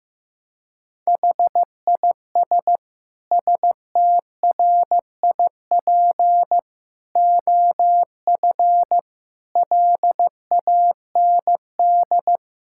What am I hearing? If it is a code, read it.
Morse code, heard as a keyed tone, about 15 wpm, 700 Hz. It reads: HIS STRIP OF LAND
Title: Sentences from Top 1600 Words 15wpm